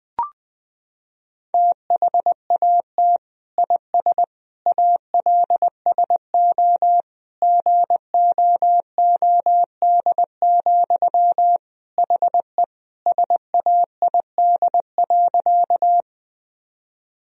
Morse code, heard as a keyed tone, about 20 wpm, 700 Hz. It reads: THAT IS ALSO GOOD, HE SAID.
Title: Sentences from Top 1000 Words - Review Part 1 - 20wpm